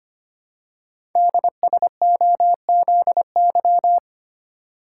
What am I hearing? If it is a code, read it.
Morse code, heard as a keyed tone, about 25 wpm, 700 Hz. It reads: DSOZY
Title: Code Groups - 5 Letters - 25wpm